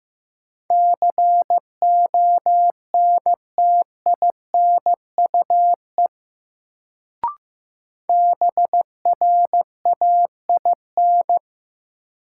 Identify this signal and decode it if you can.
Morse code, heard as a keyed tone, about 15 wpm, 700 Hz. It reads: CONTINUE BRAIN
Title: Sentences from Top 1000 Words - Review Part 6 - New Word Drill - 15wpm